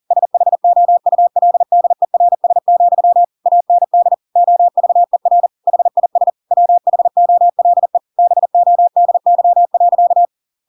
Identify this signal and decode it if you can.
Morse code, heard as a keyed tone, about 40 wpm, 700 Hz. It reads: SHOULDERS, AND OVER HIS WHOLE BODY.
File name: Pinocchio / Pinocchio ch6 40 40 ik1hge